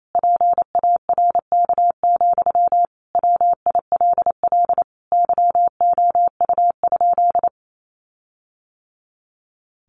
Morse code, heard as a keyed tone, about 28 wpm, 700 Hz. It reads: PARK, WILL YOU?
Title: Sentences from Top 1400 Words 28wpm